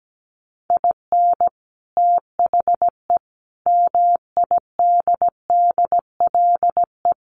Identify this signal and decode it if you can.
Morse code, heard as a keyed tone, about 17 wpm, 690 Hz. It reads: IN THE MIDDLE
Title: Sets of 5 Words Encore 17wpm